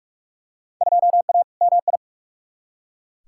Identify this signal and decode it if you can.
Morse code, heard as a keyed tone, about 42 wpm, 700 Hz. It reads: 2A MI
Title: Field Day Exchanges - 45wpm